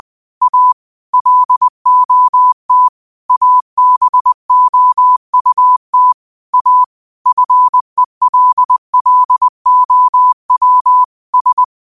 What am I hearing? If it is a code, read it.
Morse code, heard as a keyed tone, about 20 wpm, 1000 Hz. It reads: A LOT ABOUT A FELLOW S